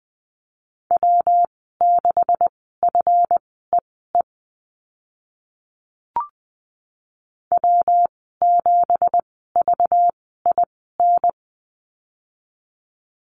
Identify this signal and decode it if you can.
Morse code, heard as a keyed tone, about 20 wpm, spaced as for 14 wpm, 700 Hz. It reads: W6FEE W7VIN
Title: 193 Lesson 31g - Letter v - Call Signs 20wpm